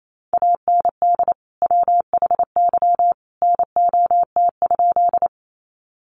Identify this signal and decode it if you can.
Morse code, heard as a keyed tone, about 28 wpm, 700 Hz. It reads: AND WHY NOT?
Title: Sentences from Top 600 Words 28wpm